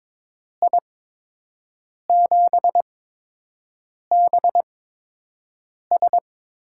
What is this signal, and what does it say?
Morse code, heard as a keyed tone, about 22 wpm, spaced as for 6 wpm, 700 Hz. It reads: I7BS
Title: USA Call Signs - 5 Characters - 8x Character Spacing - 22wpm